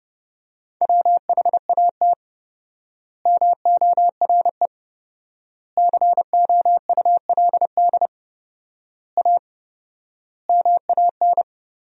Morse code, heard as a keyed tone, about 28 wpm, 700 Hz. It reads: WHAT MORE COULD A MAN
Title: 4x Word-Spacing - Sentences from Top 200 Words - 30wpm